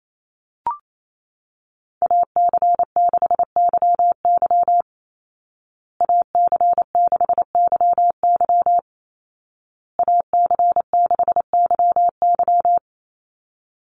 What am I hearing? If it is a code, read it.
Morse code, heard as a keyed tone, about 28 wpm, 700 Hz. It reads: AC6YY AC6YY AC6YY
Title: Call Signs Repeated 2x - 28wpm